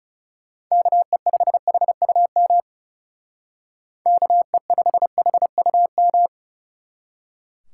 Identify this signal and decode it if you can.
Morse code, heard as a keyed tone, about 38 wpm, 700 Hz. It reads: KE5HUM KE5HUM
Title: Speed Racing - USA 2x3 Call Signs 30wpm